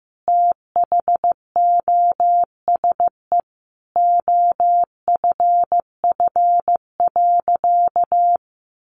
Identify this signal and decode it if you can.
Morse code, heard as a keyed tone, about 15 wpm, 700 Hz. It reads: THOSE OFF.
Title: Sentences from Top 500 Words 15wpm